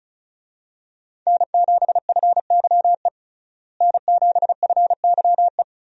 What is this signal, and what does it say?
Morse code, heard as a keyed tone, about 36 wpm, 700 Hz. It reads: N7FYE N7FYE
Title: USA Call Signs - 5 Characters - Repeated 3x - 35wpm